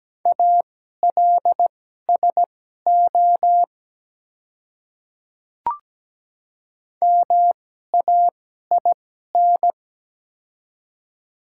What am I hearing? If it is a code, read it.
Morse code, heard as a keyed tone, about 17 wpm, spaced as for 12 wpm, 700 Hz. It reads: ALSO MAIN
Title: 2x Character Spacing - Top 300 Words - 17wpm